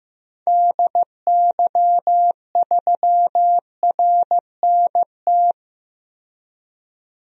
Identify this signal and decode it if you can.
Morse code, heard as a keyed tone, about 15 wpm, 700 Hz. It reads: DY3RNT
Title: DXCC-100 Call Signs - 6 Characters - 15wpm